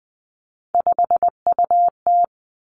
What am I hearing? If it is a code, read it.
Morse code, heard as a keyed tone, about 20 wpm, 700 Hz. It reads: 5UT
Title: Three Challenging Letter-Number Combinations 20wpm